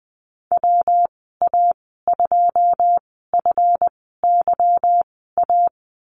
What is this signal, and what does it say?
Morse code, heard as a keyed tone, about 20 wpm, spaced as for 15 wpm, 700 Hz. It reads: WA2FYA
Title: 193 Lesson 31g - Letter v - Call Signs 20wpm